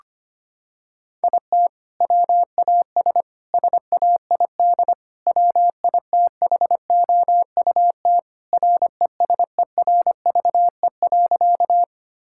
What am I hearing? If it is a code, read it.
Morse code, heard as a keyed tone, about 25 wpm, 700 Hz. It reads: IT WAS SAID WITHOUT RESERVE.
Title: Sentences from Top 1000 Words - Review Part 6 - 25wpm